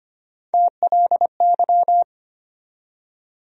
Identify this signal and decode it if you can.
Morse code, heard as a keyed tone, about 25 wpm, 700 Hz. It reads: TLY
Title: USA Call Sign Postfixes - 3 Letters 25wpm